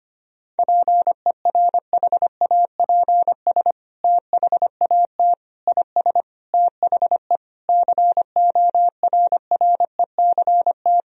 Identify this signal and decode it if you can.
Morse code, heard as a keyed tone, about 25 wpm, 700 Hz. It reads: PERHAPS THAT IS THE CORRECT
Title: Sentences from Top 1000 Words - Review Part 4 - 25wpm